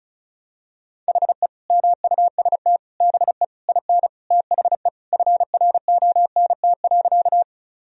Morse code, heard as a keyed tone, about 36 wpm, 700 Hz. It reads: HE MUST BE IN THE FRONT.
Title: Sentences from Top 1000 Words - Review Part 2 - 35wpm